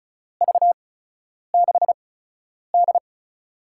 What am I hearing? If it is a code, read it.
Morse code, heard as a keyed tone, about 36 wpm, spaced as for 10 wpm, 700 Hz. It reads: V6D
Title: USA Call Signs - 4 Characters - 8x Character Spacing - 35wpm